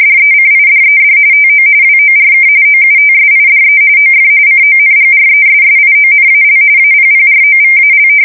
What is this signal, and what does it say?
RTTY, about 45.45 baud. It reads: DECIDED TO WORK OUTSIDE OF CONGRESSIONAL APPROVAL
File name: fsk170 / usnavy2